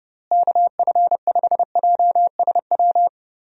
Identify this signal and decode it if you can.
Morse code, heard as a keyed tone, about 30 wpm, 700 Hz. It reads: KF5JSW
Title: USA Call Signs - 6 Characters - Repeated 3x - 30wpm